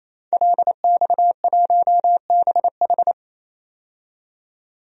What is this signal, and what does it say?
Morse code, heard as a keyed tone, about 28 wpm, 700 Hz. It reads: LX1BH
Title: DXCC-100 Call Signs - 5 Characters - 28wpm